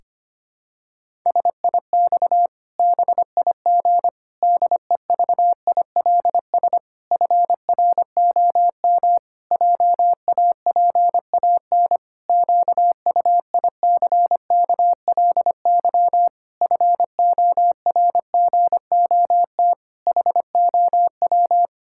Morse code, heard as a keyed tone, about 25 wpm, 700 Hz. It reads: SIX BIG DEVILS FROM JAPAN QUICKLY FORGOT HOW
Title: Pangram Challenge v2 - 25wpm